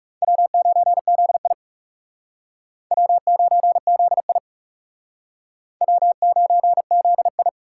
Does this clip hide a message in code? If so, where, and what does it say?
Yes: Morse code from 0.2 to 7.5 s, heard as a keyed tone, about 38 wpm, 700 Hz. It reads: W9ZI W9ZI W9ZI